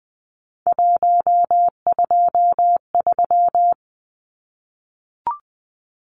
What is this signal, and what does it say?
Morse code, heard as a keyed tone, about 20 wpm, 700 Hz. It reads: 123
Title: Numbers - 3-Digit 20wpm